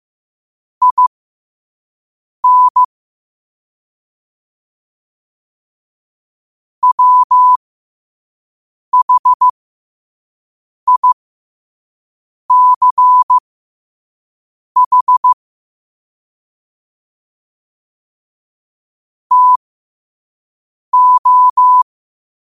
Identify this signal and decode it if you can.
Morse code, heard as a keyed tone, about 15 wpm, spaced as for 5 wpm, 1000 Hz. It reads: IN WHICH TO
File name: 5wpm